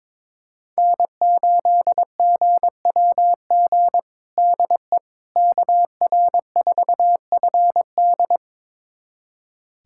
Morse code, heard as a keyed tone, about 21 wpm, 700 Hz. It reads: N8GWG DE KR4FD
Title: USA Callsign DE Callsign - 5 Characters - 22wpm